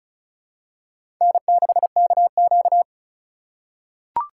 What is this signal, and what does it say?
Morse code, heard as a keyed tone, about 34 wpm, 700 Hz. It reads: N6KQ
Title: USA Call Signs - 4 Characters - 4x Character Spacing - 35wpm